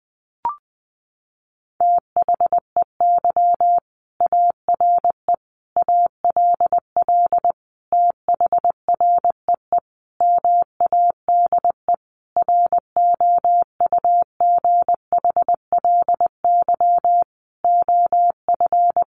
Morse code, heard as a keyed tone, about 19 wpm, 700 Hz. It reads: THEY ARE ALL THREE MADE ROUGHLY OF
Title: Sentences from Top 1500 Words 20wpm